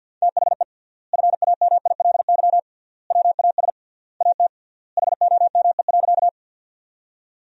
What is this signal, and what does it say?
Morse code, heard as a keyed tone, about 50 wpm, 700 Hz. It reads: THE FAMILY WAS AT HOME.
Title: 3x Word-Spacing - Sentences from Top 400 Words - 50wpm